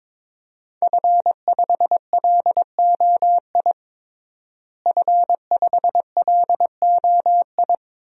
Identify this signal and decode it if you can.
Morse code, heard as a keyed tone, about 22 wpm, 700 Hz. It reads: F5LOI F5LOI
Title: DXCC-100 Call Signs - 5 Characters - Repeated 3x - 22wpm